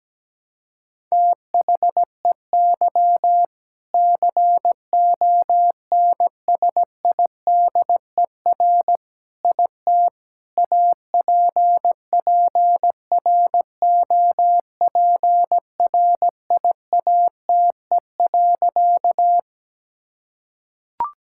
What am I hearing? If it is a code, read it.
Morse code, heard as a keyed tone, about 17 wpm, 700 Hz. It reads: THEY CONSIDER IT APPROPRIATE.
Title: Sentences from Top 1000 Words - Review Part 5 - 17wpm